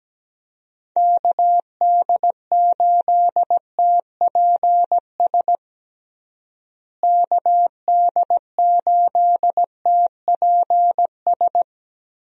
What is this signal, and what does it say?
Morse code, heard as a keyed tone, about 17 wpm, 700 Hz. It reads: KD8TPS KD8TPS
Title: USA Call Signs - 6 Characters - Repeated 2x - 17wpm